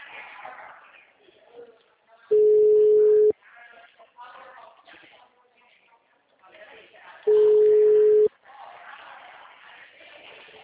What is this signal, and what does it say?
Ringback tone (Europe, 425 Hz), with speech.
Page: Dracena (Ura-Reversa)